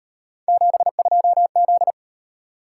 Morse code, heard as a keyed tone, about 38 wpm, 700 Hz. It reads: 72Z